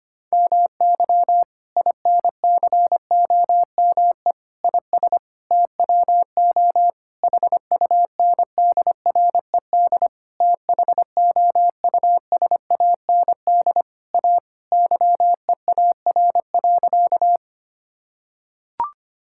Morse code, heard as a keyed tone, about 25 wpm, 700 Hz. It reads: MY INCOME IS TWO HUNDRED THOUSAND A YEAR.
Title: Sentences from Top 1000 Words - Review Part 5 - 25wpm